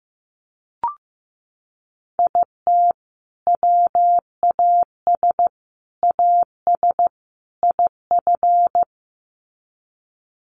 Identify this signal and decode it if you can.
Morse code, heard as a keyed tone, about 15 wpm, 700 Hz. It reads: IT WAS AS IF
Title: Sets of 4 Words 15wpm